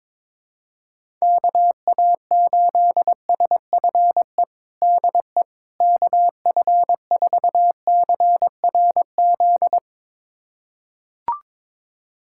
Morse code, heard as a keyed tone, about 22 wpm, 700 Hz. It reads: KA8SFE DE KF4CRZ
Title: USA Callsign DE Callsign - 6 Characters - 22wpm